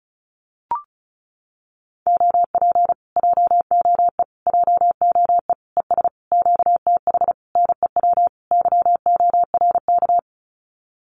Morse code, heard as a keyed tone, about 34 wpm, 700 Hz. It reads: OP JOE JOE ES QTH NEW YORK
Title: Common QSO Phrases Encore 35wpm